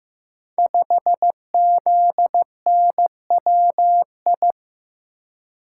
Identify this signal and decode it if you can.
Morse code, heard as a keyed tone, about 15 wpm, 700 Hz. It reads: 5ZNWI
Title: Code Groups - 5 Letter:Numbers - 15wpm